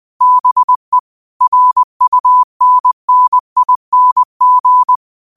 Morse code, heard as a keyed tone, about 20 wpm, 1000 Hz. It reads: BE RUNNING
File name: qotdfiles / 20wpm